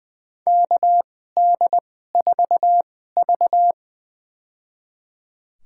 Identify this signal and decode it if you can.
Morse code, heard as a keyed tone, about 20 wpm, spaced as for 14 wpm, 700 Hz. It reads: KD4V
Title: USA Call Signs - 4 Characters - 2x Character Spacing - 20wpm